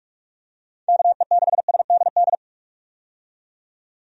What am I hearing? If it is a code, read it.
Morse code, heard as a keyed tone, about 46 wpm, 700 Hz. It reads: KE6SDD
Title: USA Call Signs - 6 Characters - 45wpm